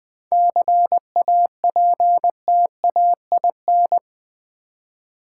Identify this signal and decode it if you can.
Morse code, heard as a keyed tone, about 20 wpm, 700 Hz. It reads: CAPTAIN
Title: 7 Letter Words - First 3 Letters Given - 20wpm